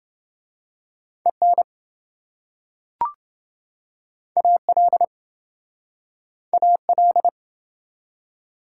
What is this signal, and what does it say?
Morse code, heard as a keyed tone, about 26 wpm, 700 Hz. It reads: EN AL AL